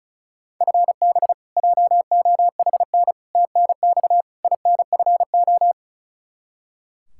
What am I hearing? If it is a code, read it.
Morse code, heard as a keyed tone, about 34 wpm, 700 Hz. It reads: FB JOHN TNX INFO
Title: Practice QSO - Exchange 3 - More Info - 35wpm